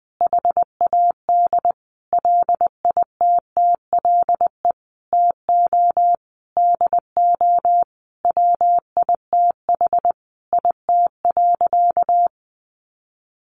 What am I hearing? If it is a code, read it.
Morse code, heard as a keyed tone, about 20 wpm, 700 Hz. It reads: HAD LITTLE TO DO WITH IT.
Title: Sentences from Top 1500 Words 20wpm